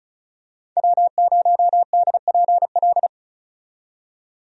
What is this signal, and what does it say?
Morse code, heard as a keyed tone, about 34 wpm, 700 Hz. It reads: W0DPL
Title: USA Call Signs - 5 Characters - 35wpm